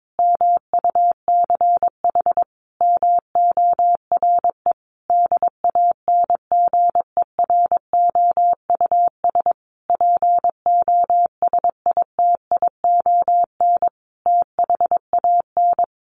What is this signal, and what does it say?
Morse code, heard as a keyed tone, about 22 wpm, 700 Hz. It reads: MUCH MORE DANGEROUS POSITION THAN
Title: Sentences from Top 1400 Words 22wpm